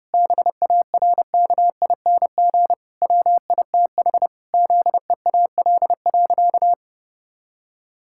Morse code, heard as a keyed tone, about 30 wpm, 700 Hz. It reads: BARKING WITH ZEAL.
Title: Pangram Challenge v2 - 30wpm